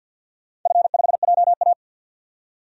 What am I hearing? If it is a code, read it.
Morse code, heard as a keyed tone, about 50 wpm, 700 Hz. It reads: F5JA
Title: DXCC-100 Call Signs - 4 Characters - 50wpm